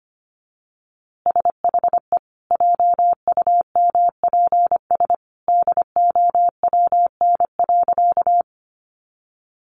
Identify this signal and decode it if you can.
Morse code, heard as a keyed tone, about 25 wpm, 700 Hz. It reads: SHE JUMPS DOWN.
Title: Easy Phrases 25wpm